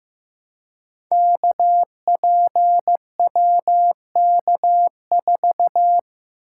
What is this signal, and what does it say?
Morse code, heard as a keyed tone, about 15 wpm, 700 Hz. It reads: KPWK4
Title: Code Groups - 5 Letter:Numbers - 15wpm